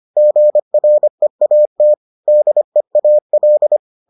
Morse code, heard as a keyed tone, about 25 wpm, 600 Hz. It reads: GREAT DEAL